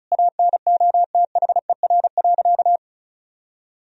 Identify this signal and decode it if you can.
Morse code, heard as a keyed tone, about 36 wpm, 700 Hz. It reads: ANOTHER.
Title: Sentences from Top 1000 Words - Review Part 1 - 35wpm